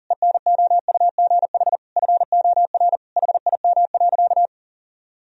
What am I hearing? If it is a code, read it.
Morse code, heard as a keyed tone, about 40 wpm, 700 Hz. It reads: ENOUGH FOR HIM.
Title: Sentences from Top 1000 Words - Review Part 2 - 40wpm